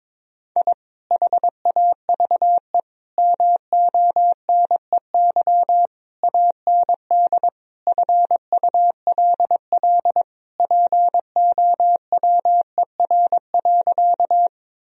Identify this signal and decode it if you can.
Morse code, heard as a keyed tone, about 22 wpm, 700 Hz. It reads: I HAVE MONEY AND FULL POWER.
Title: Sentences from Top 1000 Words - Review Part 3 - 22wpm